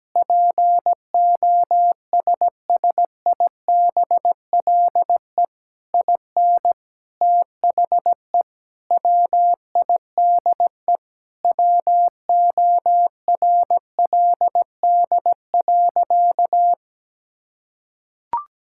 Morse code, heard as a keyed tone, about 17 wpm, 700 Hz. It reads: POSSIBLE IN THE WIDE WORLD.
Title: Sentences from Top 1000 Words - Review Part 6 - 17wpm